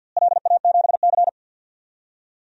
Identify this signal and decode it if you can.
Morse code, heard as a keyed tone, about 53 wpm, 700 Hz. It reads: LA7C
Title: DXCC-100 Call Signs - 4 Characters - Repeated 2x - 50wpm